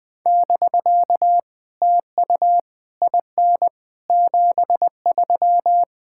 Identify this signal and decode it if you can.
Morse code, heard as a keyed tone, about 20 wpm, 700 Hz. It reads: <BK> TU IN 73
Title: POTA QSO Practice - 20wpm